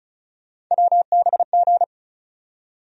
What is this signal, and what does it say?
Morse code, heard as a keyed tone, about 34 wpm, 700 Hz. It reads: WBG
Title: USA Call Sign Postfixes - 3 Letters 35wpm